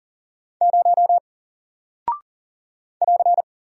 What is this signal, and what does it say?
Morse code, heard as a keyed tone, about 40 wpm, 700 Hz. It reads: 0 <AR>
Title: Single Character (Letter:Number:Punctuation:Prosign) Alt 40wpm